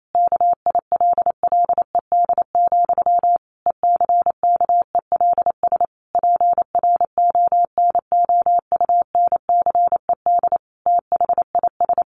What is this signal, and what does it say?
Morse code, heard as a keyed tone, about 28 wpm, 700 Hz. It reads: KILLED, ECKELS PRONOUNCED THIS
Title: A Sound Of Thunder 28wpm